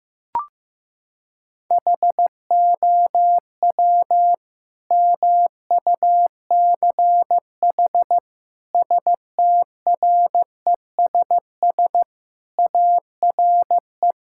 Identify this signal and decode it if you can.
Morse code, heard as a keyed tone, about 15 wpm, 700 Hz. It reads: HOW MUCH STRESS ARE
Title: Sentences from Top 1000 Words - Review Part 6 - 15wpm